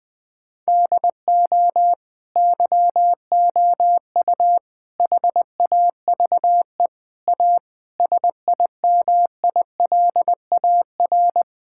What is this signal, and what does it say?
Morse code, heard as a keyed tone, about 21 wpm, 700 Hz. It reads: DO YOU HAVE A SIMILAR
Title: Sentences from Top 1000 Words - Review Part 5 - 20wpm